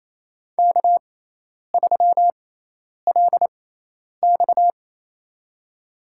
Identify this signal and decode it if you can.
Morse code, heard as a keyed tone, about 28 wpm, spaced as for 10 wpm, 700 Hz. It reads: K3LX
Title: USA Call Signs - 4 Characters - 6x Character Spacing - 28wpm